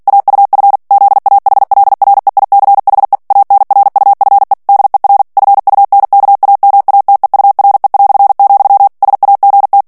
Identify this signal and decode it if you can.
Morse code, heard as a keyed tone, about 48 wpm, 800 Hz. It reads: AUF ZAHLREICHE ANRUFE DER FUNKAMATEURE", SAGT
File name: morse mp3 / ALLEMAND texte clair 2 vitesse 45 mots minute